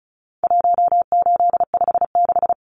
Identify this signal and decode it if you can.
Morse code, heard as a keyed tone, about 34 wpm, 700 Hz. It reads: 1856
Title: Numbers - 4-Digit 35wpm